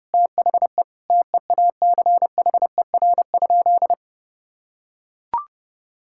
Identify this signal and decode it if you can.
Morse code, heard as a keyed tone, about 30 wpm, 700 Hz. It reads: THE TEACHER?
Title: Sentences from Top 1000 Words - Review Part 6 - 30wpm